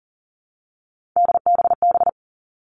Morse code, heard as a keyed tone, about 40 wpm, 700 Hz. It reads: DBB
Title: All Cut Numbers - 3-Digit 40wpm